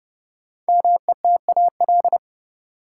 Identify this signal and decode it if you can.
Morse code, heard as a keyed tone, about 30 wpm, 700 Hz. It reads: METAL
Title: Sentences from Top 1600 Words - New Word Drill Alt - 30wpm